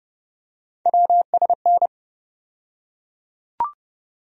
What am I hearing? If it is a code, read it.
Morse code, heard as a keyed tone, about 31 wpm, 700 Hz. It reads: WSN
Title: USA Call Sign Postfixes - 3 Letters 30wpm